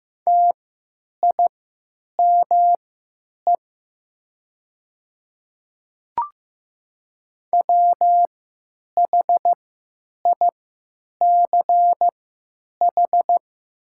Morse code, heard as a keyed tone, about 15 wpm, spaced as for 9 wpm, 700 Hz. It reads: TIME WHICH
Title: 3x Character Spacing - Top 200 Words - 15wpm